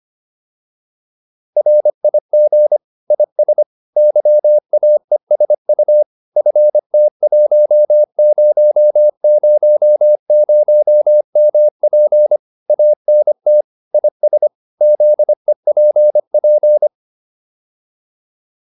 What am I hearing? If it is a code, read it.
Morse code, heard as a keyed tone, about 25 wpm, 600 Hz. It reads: RIG IS YAESU FT1000MP ANT IS ZEPP